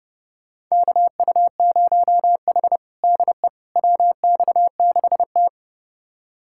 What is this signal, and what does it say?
Morse code, heard as a keyed tone, about 30 wpm, 700 Hz. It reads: KU0H DE WX6T
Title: USA Callsign DE Callsign - 4 Characters - 30wpm